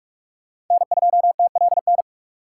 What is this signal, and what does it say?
Morse code, heard as a keyed tone, about 44 wpm, 700 Hz. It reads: N2TLN
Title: USA Call Signs - 6x Character Spacing - 45wpm